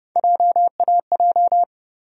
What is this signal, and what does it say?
Morse code, heard as a keyed tone, about 30 wpm, 700 Hz. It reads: JAJ
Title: USA Call Sign Postfixes - 3 Letters 30wpm